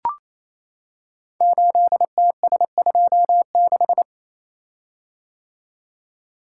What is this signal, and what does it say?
Morse code, heard as a keyed tone, about 27 wpm, 700 Hz. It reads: 8TS26
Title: Code Groups - 5 Letter:Numbers - 28wpm